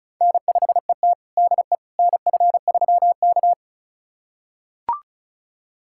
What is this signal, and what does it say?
Morse code, heard as a keyed tone, about 36 wpm, 700 Hz. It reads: N5ET DE NF3K
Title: USA Callsign DE Callsign - 4 Characters - 35wpm